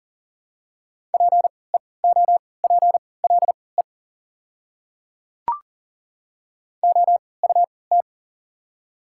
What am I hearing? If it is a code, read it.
Morse code, heard as a keyed tone, about 40 wpm, spaced as for 23 wpm, 700 Hz. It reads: PEOPLE OUT
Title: 3x Character Spacing - Top 200 Words - 40wpm